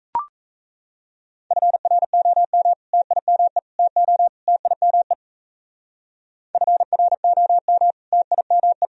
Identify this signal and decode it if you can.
Morse code, heard as a keyed tone, about 37 wpm, 700 Hz. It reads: FROM TIME TO TIME FROM TIME